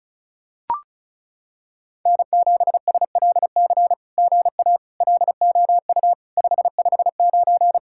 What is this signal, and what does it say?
Morse code, heard as a keyed tone, about 36 wpm, 700 Hz. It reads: N7SLC GA LOU 559
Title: SOTA QSO Practice 35wpm